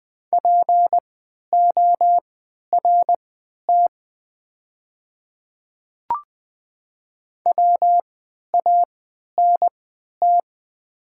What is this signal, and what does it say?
Morse code, heard as a keyed tone, about 20 wpm, spaced as for 11 wpm, 690 Hz. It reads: PORT WANT
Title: 3x Character Spacing - Top 500 Words - 20wpm